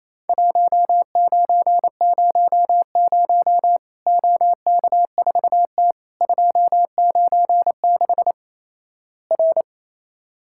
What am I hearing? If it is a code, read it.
Morse code, heard as a keyed tone, about 28 wpm, 700 Hz. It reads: 1900 OK4T 296 R
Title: Worked All Europe DX Contest - QTC Exchange - 28wpm